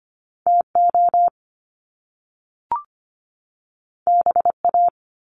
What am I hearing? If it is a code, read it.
Morse code, heard as a keyed tone, about 25 wpm, 700 Hz. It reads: TO BA
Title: Top Two Letter Combinations in Words 25wpm